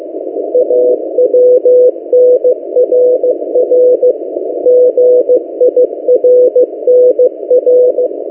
Morse code, heard as a keyed tone, about 15 wpm, 500 Hz. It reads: AWNRR GIRNR
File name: M08a- -1355utc-8096khz